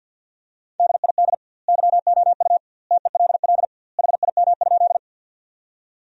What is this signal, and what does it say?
Morse code, heard as a keyed tone, about 52 wpm, 700 Hz. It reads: DID YOU TELL HIM?
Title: 2x Word-Spacing - Sentences from Top 200 Words - 50wpm